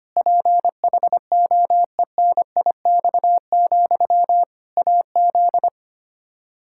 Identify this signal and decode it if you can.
Morse code, heard as a keyed tone, about 25 wpm, 700 Hz. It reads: PHOENIX, AZ
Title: Practice QSO - Exchange 1 - Beginning - 25wpm